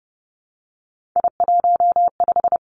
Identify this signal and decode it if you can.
Morse code, heard as a keyed tone, about 30 wpm, 700 Hz. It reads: I15
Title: Three Challenging Letter-Number Combinations 30wpm